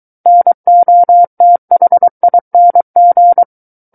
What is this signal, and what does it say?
Morse code, heard as a keyed tone, about 23 wpm, 700 Hz. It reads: NOTHING
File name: Book One / 14-Chapter 14 - 23 WPM